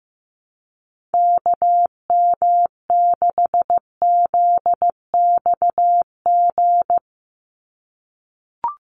Morse code, heard as a keyed tone, about 15 wpm, 700 Hz. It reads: KM6ZXG
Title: USA Call Signs 15wpm